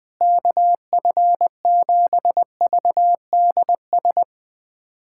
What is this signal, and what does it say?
Morse code, heard as a keyed tone, about 20 wpm, 700 Hz. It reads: KF7VDS
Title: USA Callsign DE Callsign - 6 Characters - 20wpm